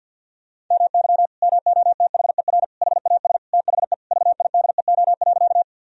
Morse code, heard as a keyed tone, about 50 wpm, 700 Hz. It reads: MY MOTHER HAS THE VIDEO.
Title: Sentences from Top 1000 Words - Review Part 4 - 50wpm